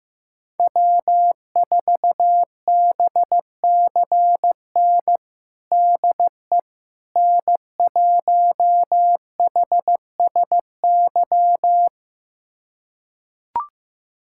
Morse code, heard as a keyed tone, about 15 wpm, 700 Hz. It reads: W4BCN DE N1HSY
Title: USA Callsign DE Callsign - 5 Characters - 15wpm